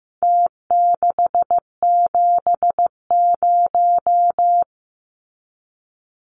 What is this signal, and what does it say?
Morse code, heard as a keyed tone, about 15 wpm, 700 Hz. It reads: T670
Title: Numbers - Contest Style - 4-Digit 15wpm